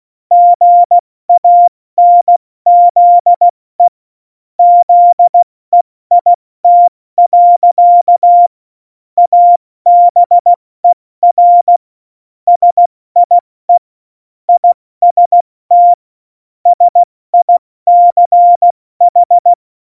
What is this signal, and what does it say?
Morse code, heard as a keyed tone, about 16 wpm, 700 Hz. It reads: GANZE ZEIT. ABER SIE IST SICH
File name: Morsekurs / Witze / Witz 23. Juli 2019, 70 F80